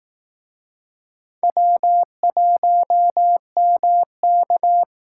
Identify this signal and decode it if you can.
Morse code, heard as a keyed tone, about 18 wpm, 700 Hz. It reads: W1MK
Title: Speed Racing - USA Call Signs 15wpm